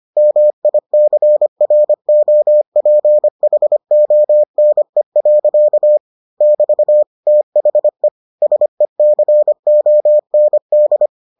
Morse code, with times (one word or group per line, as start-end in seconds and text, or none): 0.2-6.0 MICROPHONE.
6.4-7.0 <BT>
7.3-8.1 THE
8.4-11.1 SECOND